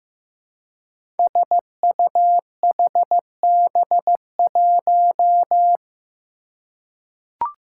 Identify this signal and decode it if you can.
Morse code, heard as a keyed tone, about 15 wpm, 700 Hz. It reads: SUHB1
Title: Five Challenging Letter-Number Combinations 15wpm